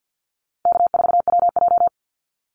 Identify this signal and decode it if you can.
Morse code, heard as a keyed tone, about 50 wpm, 700 Hz. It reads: K4WJ
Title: Easy Call Signs - 50wpm